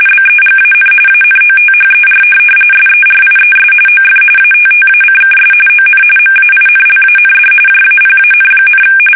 RTTY, about 45.45 baud. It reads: BEGAN ANOTHER SERIES OF ATTACKS ON MAINLAND JAPAN,
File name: fsk850 / USS-Massachusetts